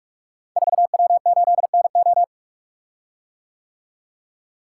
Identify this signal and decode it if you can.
Morse code, heard as a keyed tone, about 44 wpm, 700 Hz. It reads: 4W8NO
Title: Code Groups - 5 Letter:Numbers - 45wpm